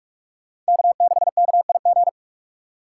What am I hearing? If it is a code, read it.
Morse code, heard as a keyed tone, about 58 wpm, 700 Hz. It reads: K6KIG